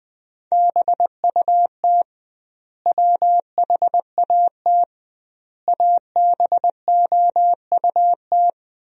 Morse code, heard as a keyed tone, about 20 wpm, 700 Hz. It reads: BUT WHAT ABOUT
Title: 2x Word-Spacing - Sentences from Top 200 Words - 20wpm